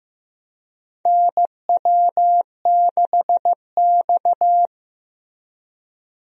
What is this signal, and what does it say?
Morse code, heard as a keyed tone, about 16 wpm, 700 Hz. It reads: NW6X
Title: USA Call Signs - 4 Characters - Repeated 2x - 15wpm